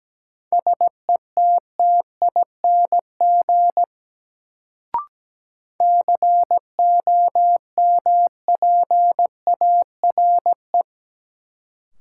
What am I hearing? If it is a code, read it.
Morse code, heard as a keyed tone, about 17 wpm, 700 Hz. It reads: SETTING COMPARE
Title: Sentences from Top 1300 Words - New Word Drill - 17wpm